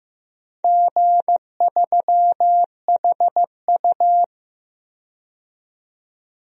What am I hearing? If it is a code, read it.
Morse code, heard as a keyed tone, about 15 wpm, 700 Hz. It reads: G3HU
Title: DXCC-100 Call Signs - 4 Characters - 15wpm